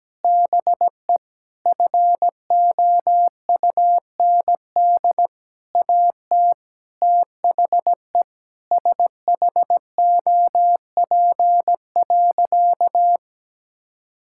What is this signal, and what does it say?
Morse code, heard as a keyed tone, about 17 wpm, 700 Hz. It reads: BE FOUND AT THE SHOP.
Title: Sentences from Top 1600 Words 17wpm